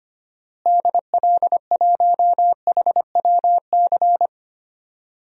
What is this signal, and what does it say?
Morse code, heard as a keyed tone, about 25 wpm, 700 Hz. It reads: DL1HWC
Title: DXCC-100 Call Signs - 6 Characters - Repeated 3x - 25wpm